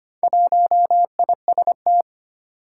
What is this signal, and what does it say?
Morse code, heard as a keyed tone, about 25 wpm, 700 Hz. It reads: 1IST
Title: Four Challenging Letter-Number Combinations 25wpm